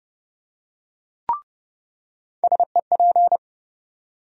Morse code, heard as a keyed tone, about 30 wpm, 700 Hz. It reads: SEP
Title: USA Call Sign Postfixes - 3 Letters 30wpm